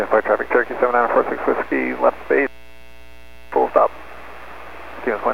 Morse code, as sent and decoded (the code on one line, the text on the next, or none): none